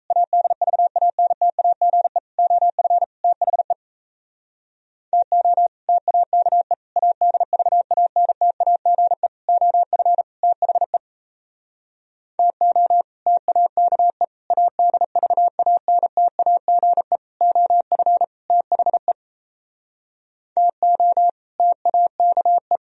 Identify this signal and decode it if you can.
Morse code, heard as a keyed tone, about 36 wpm, 700 Hz. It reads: ADVANTAGE OF THE TO TAKE ADVANTAGE OF THE TO TAKE ADVANTAGE OF THE TO TAKE